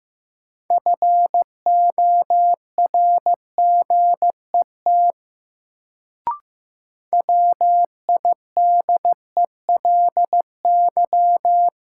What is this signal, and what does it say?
Morse code, heard as a keyed tone, about 15 wpm, 700 Hz. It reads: FORGET WIDELY
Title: Sentences from Top 1200 Words - New Word Drill Alt - 15wpm